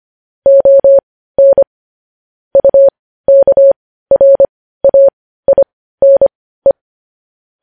Morse code, heard as a keyed tone, about 25 wpm, spaced as for 15 wpm, 550 Hz. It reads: ON UKRAINE
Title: News Headlines at 15